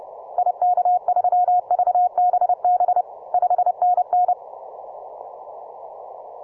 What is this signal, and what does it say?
Morse code, heard as a keyed tone, about 30 wpm, 720 Hz. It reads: IK3VBB 5NN